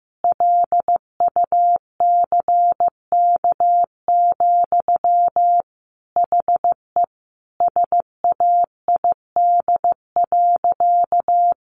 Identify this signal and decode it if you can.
Morse code, heard as a keyed tone, about 15 wpm, 700 Hz. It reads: LUCK, HE SAID.
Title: Sentences from Top 1500 Words 15wpm